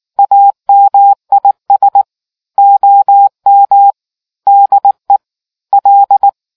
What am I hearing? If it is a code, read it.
Morse code, heard as a keyed tone, about 19 wpm, 800 Hz. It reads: AMIS OM DE L